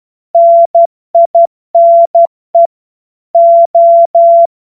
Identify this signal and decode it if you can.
Morse code, heard as a keyed tone, about 12 wpm, 680 Hz. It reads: NINE O
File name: Morsekurs / Witze